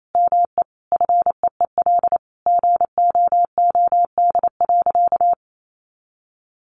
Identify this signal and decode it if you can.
Morse code, heard as a keyed tone, about 28 wpm, 700 Hz. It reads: ME FEEL GOOD.
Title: Sentences from Top 600 Words 28wpm